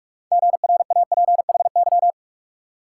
Morse code, heard as a keyed tone, about 60 wpm, spaced as for 52 wpm, 700 Hz. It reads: GRAPHY
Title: Speed-Racing Multi-letter Phonemes - 45wpm